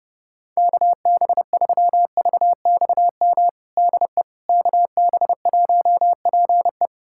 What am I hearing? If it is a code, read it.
Morse code, heard as a keyed tone, about 32 wpm, 700 Hz. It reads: KB3VXM DE KB1PE
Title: USA Callsign DE Callsign 30wpm